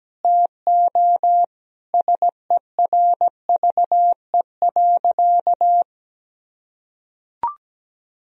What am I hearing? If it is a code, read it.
Morse code, heard as a keyed tone, about 17 wpm, 700 Hz. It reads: TO SERVE.
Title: Sentences from Top 1000 Words - Review Part 4 - 17wpm